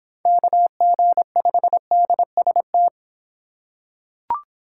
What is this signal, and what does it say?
Morse code, heard as a keyed tone, about 26 wpm, 700 Hz. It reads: KG5DST